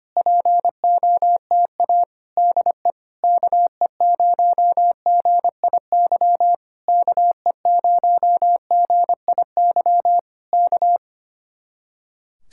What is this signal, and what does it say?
Morse code, heard as a keyed tone, about 25 wpm, 700 Hz. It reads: POTA DE KE0GIY KE0GIY K
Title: Speed-Racing Common QSO Phrases - 25wpm